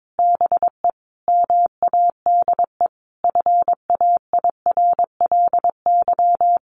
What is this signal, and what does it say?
Morse code, heard as a keyed tone, about 22 wpm, 700 Hz. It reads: BE MADE FAIRLY
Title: Sentences from Top 1100 Words 22wpm